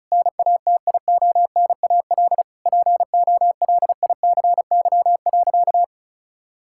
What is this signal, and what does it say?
Morse code, heard as a keyed tone, about 34 wpm, 700 Hz. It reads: NATIONAL POLICY.
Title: Sentences from Top 1000 Words - Review Part 6 - 35wpm